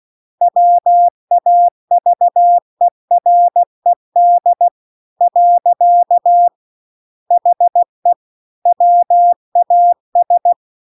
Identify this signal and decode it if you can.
Morse code, heard as a keyed tone, about 16 wpm, 700 Hz. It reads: WAVERED . HE WAS
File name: CW-The-Big-Twoheaded-River-2-16WPM